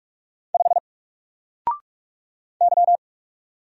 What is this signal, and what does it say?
Morse code, heard as a keyed tone, about 46 wpm, 700 Hz. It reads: 5 Y
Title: Single Letter-Number Alt 45wpm